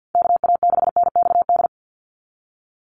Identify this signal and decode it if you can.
Morse code, heard as a keyed tone, about 50 wpm, 700 Hz. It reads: KA6NXD
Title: USA Call Signs 50wpm